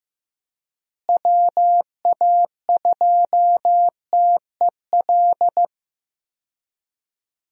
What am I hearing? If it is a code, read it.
Morse code, heard as a keyed tone, about 15 wpm, 700 Hz. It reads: WA2TEL
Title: USA Call Signs - 6 Characters - Repeated 2x - 15wpm